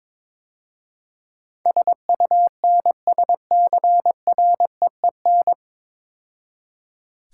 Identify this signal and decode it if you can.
Morse code, heard as a keyed tone, about 22 wpm, 700 Hz. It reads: SUNSCREEN
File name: Easy Words - Speed-Racing with Hints - 22wpm